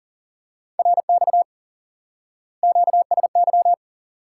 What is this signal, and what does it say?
Morse code, heard as a keyed tone, about 37 wpm, 700 Hz. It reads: RX QSY
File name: IQE Course 3 - Lesson 89 - YAGI - Two Elements - 40wpm